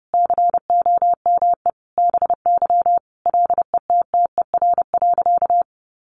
Morse code, heard as a keyed tone, about 30 wpm, 700 Hz. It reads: COME BY LETTER.
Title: Sentences from Top 1400 Words 30wpm